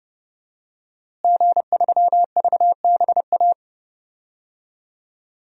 Morse code, heard as a keyed tone, about 30 wpm, 700 Hz. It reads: G3VBA
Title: DXCC-100 Call Signs - 5 Characters - Repeated 3x - 30wpm